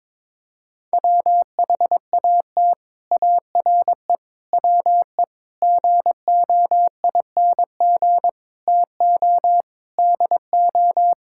Morse code, heard as a keyed tone, about 22 wpm, 700 Hz. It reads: WHAT ARE WE GOING TO DO
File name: practice / 22wpm / Sets of 6 Words - 22wpm